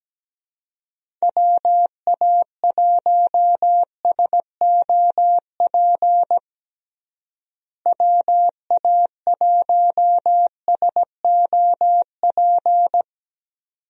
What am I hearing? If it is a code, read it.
Morse code, heard as a keyed tone, about 17 wpm, 700 Hz. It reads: WA1SOP WA1SOP
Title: USA Call Signs - 6 Characters - Repeated 2x - 17wpm